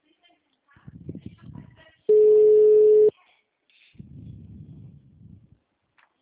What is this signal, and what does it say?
Ringback tone (Europe, 425 Hz), with speech.